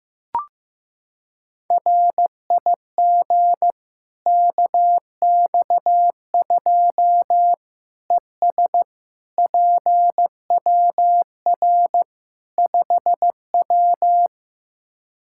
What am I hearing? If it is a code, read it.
Morse code, heard as a keyed tone, about 16 wpm, 700 Hz. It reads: RIG KX2 ES PWR 5W
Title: Practice QSO - Exchange 2 - Info - 15wpm